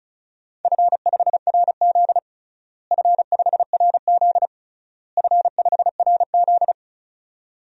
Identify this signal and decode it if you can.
Morse code, heard as a keyed tone, about 36 wpm, 690 Hz. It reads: F5RZ F5RZ F5RZ
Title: DXCC-100 Call Signs - 4 Characters - Repeated 3x - 35wpm